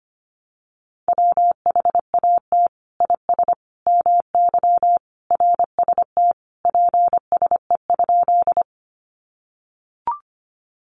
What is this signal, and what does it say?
Morse code, heard as a keyed tone, about 25 wpm, 700 Hz. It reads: WHAT IS MY RST PSE?
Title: Common QSO Phrases 25wpm